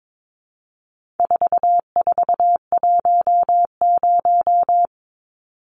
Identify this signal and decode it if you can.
Morse code, heard as a keyed tone, about 22 wpm, 700 Hz. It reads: 4410
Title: Numbers - 4-Digit 22wpm